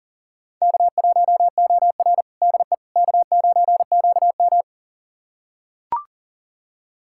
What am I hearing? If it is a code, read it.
Morse code, heard as a keyed tone, about 40 wpm, 700 Hz. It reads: K1OR DE K9QM
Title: USA Callsign DE Callsign - 4 Characters - 40wpm